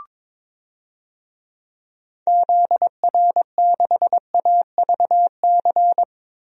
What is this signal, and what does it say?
Morse code, heard as a keyed tone, about 22 wpm, 700 Hz. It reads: ZR6AVC
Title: DXCC-100 Call Signs - 6 Characters - 22wpm